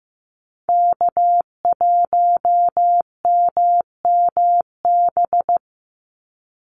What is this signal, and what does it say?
Morse code, heard as a keyed tone, about 15 wpm, 700 Hz. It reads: K1MMB
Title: Call Signs Repeated 2x - 15wpm